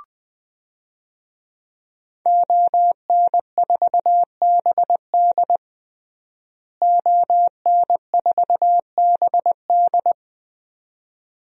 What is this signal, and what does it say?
Morse code, heard as a keyed tone, about 20 wpm, 700 Hz. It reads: ON4BD ON4BD
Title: DXCC-100 Call Signs - 5 Characters - Repeated 2x - 20wpm